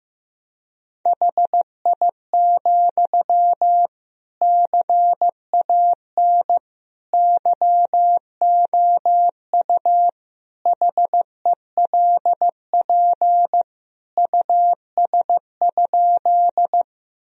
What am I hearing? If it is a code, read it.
Morse code, heard as a keyed tone, about 16 wpm, 700 Hz. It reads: HI, CAN YOU HELP US?
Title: Sentences from Top 1000 Words - Review Part 3 - 15wpm